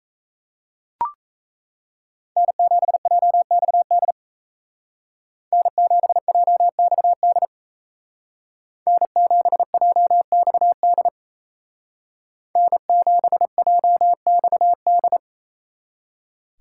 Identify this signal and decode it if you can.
Morse code, heard as a keyed tone, about 34 wpm, 700 Hz. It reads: N7JXD N7JXD N7JXD N7JXD
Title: Speed Racing - USA 1x3 Call Signs 28wpm